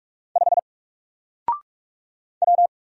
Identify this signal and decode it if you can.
Morse code, heard as a keyed tone, about 44 wpm, 700 Hz. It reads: 5 W
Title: Single Character (Letter:Number:Punctuation:Prosign) Alt 45wpm